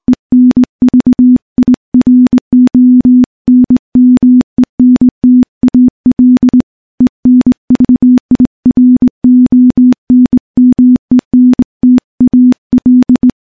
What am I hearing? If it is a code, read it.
Morse code, heard as a keyed tone, about 20 wpm, 250 Hz. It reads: ENVIRONMENTAL ENVIRONMENTAL
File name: morse / morse-training-20WPM